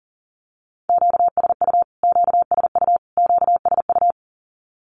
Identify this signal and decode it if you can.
Morse code, heard as a keyed tone, about 40 wpm, 700 Hz. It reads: QSU QSU QSU
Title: Q-Codes Repeated 3x 40wpm